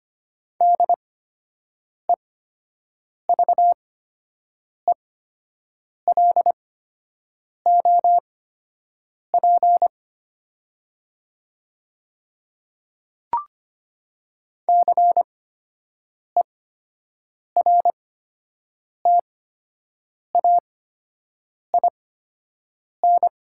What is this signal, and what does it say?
Morse code, heard as a keyed tone, about 25 wpm, spaced as for 7 wpm, 700 Hz. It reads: DEVELOP CERTAIN
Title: 8x Character Spacing - Top 500 Words - 25wpm